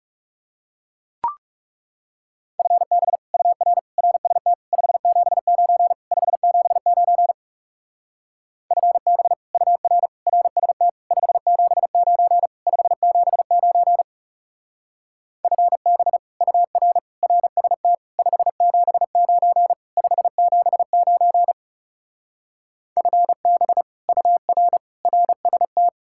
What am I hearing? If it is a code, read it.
Morse code, heard as a keyed tone, about 40 wpm, 700 Hz. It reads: FB UR RST 579 579 FB UR RST 579 579 FB UR RST 579 579 FB UR RST